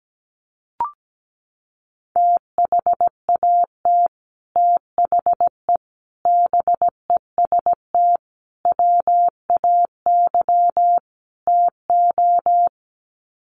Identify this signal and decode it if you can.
Morse code, heard as a keyed tone, about 17 wpm, 700 Hz. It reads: THAT THE BEST WAY TO
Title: Sets of 5 Words Encore2 17wpm